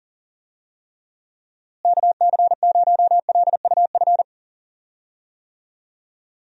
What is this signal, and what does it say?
Morse code, heard as a keyed tone, about 40 wpm, 700 Hz. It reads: KC0LUF
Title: USA Call Signs - 6 Characters - Repeated 2x - 40wpm